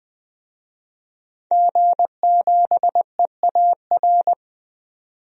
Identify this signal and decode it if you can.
Morse code, heard as a keyed tone, about 21 wpm, 700 Hz. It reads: G7EAR
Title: DXCC-100 Call Signs - 5 Characters - 20wpm